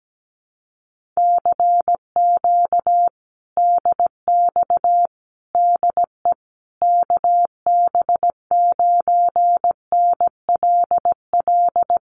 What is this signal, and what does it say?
Morse code, heard as a keyed tone, about 17 wpm, 700 Hz. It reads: CQ DX DE KB9NLL
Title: Common QSO Phrases Encore 17wpm